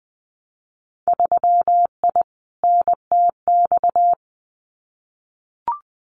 Morse code, heard as a keyed tone, about 20 wpm, 700 Hz. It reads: 3I NTX
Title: Winter Field Day Exchanges 20wpm